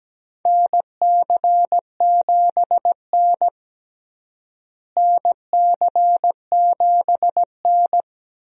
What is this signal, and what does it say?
Morse code, heard as a keyed tone, about 17 wpm, 700 Hz. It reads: NC7N NC7N
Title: USA Call Signs - 4 Characters - Repeated 3x - 17wpm